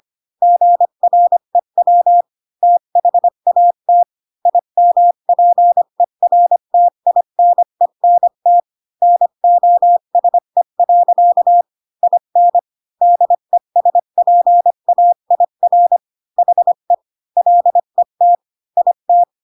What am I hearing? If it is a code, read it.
Morse code, heard as a keyed tone, about 25 wpm, 700 Hz. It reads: GREW THAT IMPERTINENT NOSE. IN DESPAIR HE LET IT
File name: Pinocchio / Pinocchio ch3 20 25 ik1hge